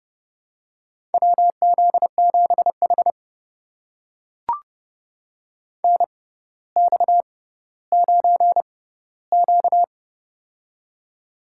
Morse code, heard as a keyed tone, about 31 wpm, spaced as for 15 wpm, 700 Hz. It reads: WZ7H NX9Q
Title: USA Call Signs - 4 Characters - 6x Character Spacing - 30wpm